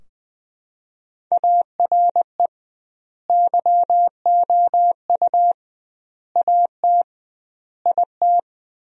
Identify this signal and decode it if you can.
Morse code, heard as a keyed tone, about 20 wpm, 700 Hz. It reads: ARE YOU AT IT
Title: 2x Word-Spacing - Sentences from Top 200 Words - 20wpm